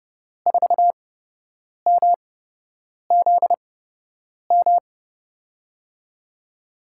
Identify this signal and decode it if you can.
Morse code, heard as a keyed tone, about 30 wpm, spaced as for 8 wpm, 700 Hz. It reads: 4MZM
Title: USA Call Signs - 8x Character Spacing - 30wpm